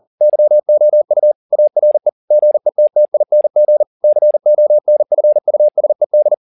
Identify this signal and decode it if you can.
Morse code, heard as a keyed tone, about 40 wpm, 600 Hz. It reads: YOU ARE GETTING CONFUSED
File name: the old man and the sea 2 40wpm